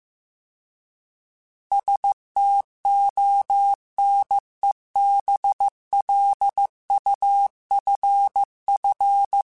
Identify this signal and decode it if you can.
Morse code, heard as a keyed tone, about 15 wpm, 800 Hz. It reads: STONE BLUFF